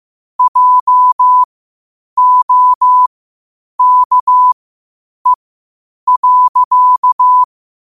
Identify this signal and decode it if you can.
Morse code, heard as a keyed tone, about 15 wpm, spaced as for 8 wpm, 1000 Hz. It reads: JOKE.
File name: qotdfiles / 8wpm